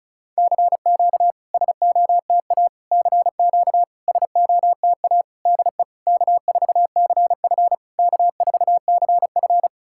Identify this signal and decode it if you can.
Morse code, heard as a keyed tone, about 34 wpm, 690 Hz. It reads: CQ SOTA CQ SOTA DE K4CF K4CF
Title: SOTA QSO Practice 35wpm